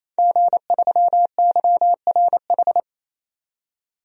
Morse code, heard as a keyed tone, about 28 wpm, 700 Hz. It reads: G3YRH
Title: DXCC-100 Call Signs - 5 Characters - 28wpm